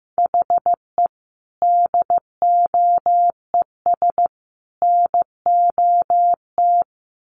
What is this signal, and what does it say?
Morse code, heard as a keyed tone, about 15 wpm, 700 Hz. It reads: HE DOES NOT
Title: Sets of 3 Words 15wpm